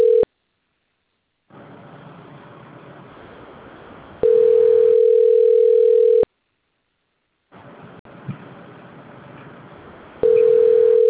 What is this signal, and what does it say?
Ringback tone (North America), 440 + 480 Hz.